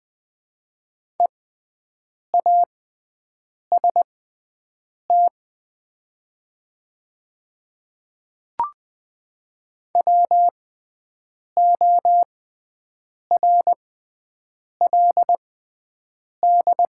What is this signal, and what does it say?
Morse code, heard as a keyed tone, about 21 wpm, spaced as for 7 wpm, 700 Hz. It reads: EAST WORLD
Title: 6x Character Spacing - Top 500 Words - 20wpm